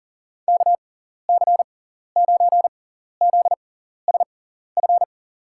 Morse code, heard as a keyed tone, about 40 wpm, spaced as for 14 wpm, 700 Hz. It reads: KC9ZSF
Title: USA Call Signs - 6 Characters - 6x Character Spacing - 40wpm